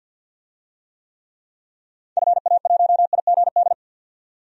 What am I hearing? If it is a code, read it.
Morse code, heard as a keyed tone, about 47 wpm, 700 Hz. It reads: FA1IGD
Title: DXCC-100 Call Signs - 6 Characters - 50wpm